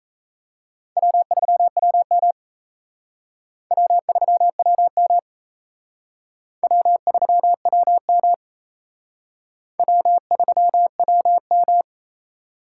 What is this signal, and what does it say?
Morse code, heard as a keyed tone, about 35 wpm, 700 Hz. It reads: W3WM W3WM W3WM W3WM
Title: Speed Racing - USA 1x2 Call Signs 28wpm